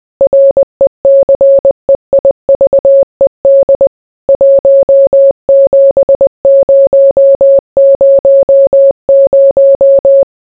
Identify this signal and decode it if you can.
Morse code, heard as a keyed tone, about 20 wpm, 550 Hz. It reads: RECEIVED 17000